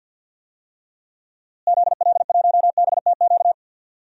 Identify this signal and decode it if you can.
Morse code, heard as a keyed tone, about 48 wpm, 700 Hz. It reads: ZL1BTQ
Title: DXCC-100 Call Signs - 6 Characters - Repeated 2x - 50wpm